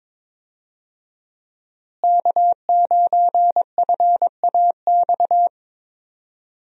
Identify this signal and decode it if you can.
Morse code, heard as a keyed tone, about 22 wpm, 700 Hz. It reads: K9FAX
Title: USA Call Signs - 5 Characters - Repeated 3x - 22wpm